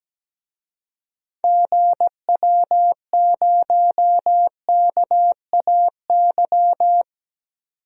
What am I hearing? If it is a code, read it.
Morse code, heard as a keyed tone, about 17 wpm, 700 Hz. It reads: GW0KAY
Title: DXCC-100 Call Signs - 6 Characters - 17wpm